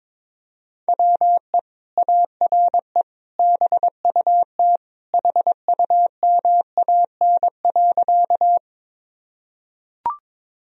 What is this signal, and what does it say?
Morse code, heard as a keyed tone, about 22 wpm, 700 Hz. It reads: WE ARE BUT HUMAN.
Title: Sentences from Top 1000 Words - Review Part 3 - 22wpm